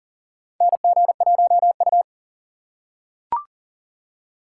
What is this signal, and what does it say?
Morse code, heard as a keyed tone, about 38 wpm, 690 Hz. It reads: NG1U
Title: USA Call Signs - 4 Characters - 2x Character Spacing - 40wpm